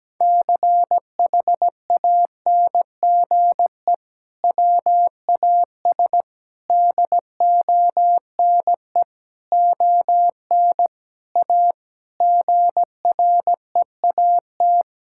Morse code, heard as a keyed tone, about 17 wpm, 700 Hz. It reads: CHANGE WAS DONE ON A GREAT
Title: Sentences from Top 1000 Words - Review Part 5 - 17wpm